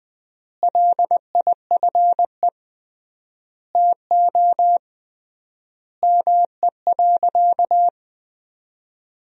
Morse code, heard as a keyed tone, about 21 wpm, 700 Hz. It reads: LIFE TO ME.
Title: 3x Word-Spacing - Sentences from Top 500 Words - 20wpm